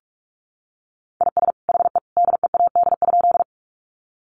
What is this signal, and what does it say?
Morse code, heard as a keyed tone, about 45 wpm, 700 Hz. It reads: IS HE DEAD?
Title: Sentences from Top 1500 Words 45wpm